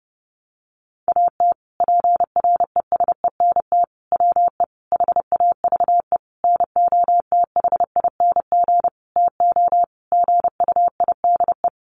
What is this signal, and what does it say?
Morse code, heard as a keyed tone, about 30 wpm, 700 Hz. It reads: AT PRESENT WE HAVE NOTHING TO GUIDE
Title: Sentences from Top 800 Words 30wpm